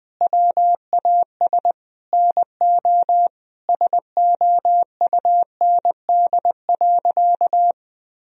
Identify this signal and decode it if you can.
Morse code, heard as a keyed tone, about 20 wpm, 700 Hz. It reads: WAS NO SOUND.
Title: Sentences from Top 1000 Words - Review Part 2 - 20wpm